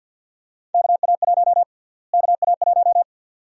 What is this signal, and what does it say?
Morse code, heard as a keyed tone, about 51 wpm, 700 Hz. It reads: KA1 KA1
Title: USA Call Sign Prefixes - Repeated 2x - 50wpm